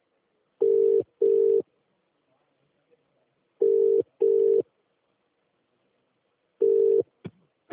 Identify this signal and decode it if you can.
Ringback tone (UK), 400 + 450 Hz.